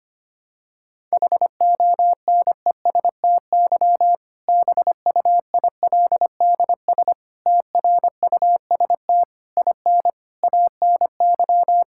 Morse code, heard as a keyed tone, about 25 wpm, 700 Hz. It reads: HONESTY BUILDS TRUST IN ANY
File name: practice / 25wpm / Difficult Sentences with Hints - 25wpm